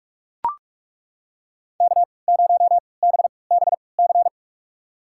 Morse code, heard as a keyed tone, about 42 wpm, spaced as for 25 wpm, 700 Hz. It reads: K0BBC
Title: USA Call Signs - 3x Character Spacing - 45wpm